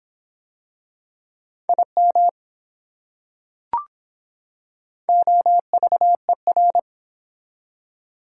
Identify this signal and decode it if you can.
Morse code, heard as a keyed tone, about 24 wpm, 700 Hz. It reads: IM OVER